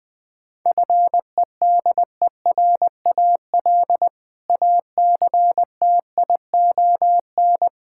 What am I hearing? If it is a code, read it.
Morse code, heard as a keyed tone, about 20 wpm, 700 Hz. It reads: FEDERAL ACTION
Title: Sentences from Top 1000 Words - Review Part 5 - 20wpm